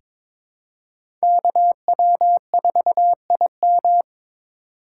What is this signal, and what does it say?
Morse code, heard as a keyed tone, about 22 wpm, 700 Hz. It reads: KW4IM
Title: USA Call Signs - 5 Characters - 3x Character Spacing - 22wpm